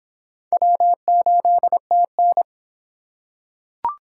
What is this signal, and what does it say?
Morse code, heard as a keyed tone, about 25 wpm, 700 Hz. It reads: W8TN